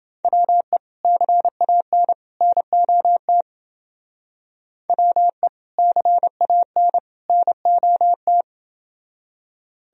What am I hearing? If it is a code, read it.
Morse code, heard as a keyed tone, about 29 wpm, 700 Hz. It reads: WE CAN NOT WE CAN NOT